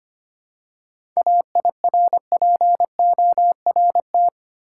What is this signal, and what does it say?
Morse code, heard as a keyed tone, about 25 wpm, 700 Hz. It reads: AIRPORT
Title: Seven Letter Words - 25wpm